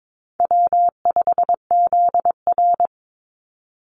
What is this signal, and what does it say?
Morse code, heard as a keyed tone, about 22 wpm, 700 Hz. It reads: W5ZR
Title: Easy Call Signs Repeated 3x - 22wpm